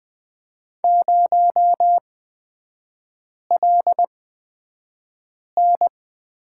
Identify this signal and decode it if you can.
Morse code, heard as a keyed tone, about 20 wpm, spaced as for 5 wpm, 700 Hz. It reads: 0LN
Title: 250 Lesson 40e - BK - Three Characters 20wpm